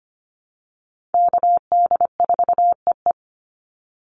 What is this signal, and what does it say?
Morse code, heard as a keyed tone, about 25 wpm, 700 Hz. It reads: KD4EE
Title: Call Signs Repeated 3x - 25wpm